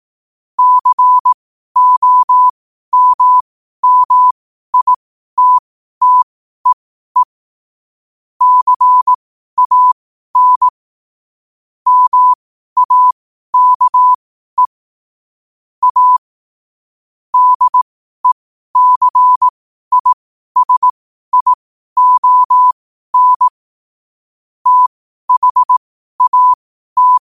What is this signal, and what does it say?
Morse code, heard as a keyed tone, about 18 wpm, spaced as for 12 wpm, 1000 Hz. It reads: COMMITTEE CAN MAKE A DECISION THAT
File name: qotdfiles / 12wpm